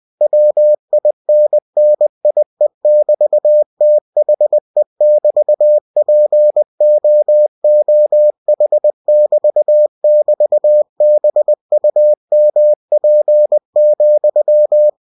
Morse code, heard as a keyed tone, about 20 wpm, 600 Hz. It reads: WINNIE<BT>THE<BT>POOH<BT><BT>BUMP,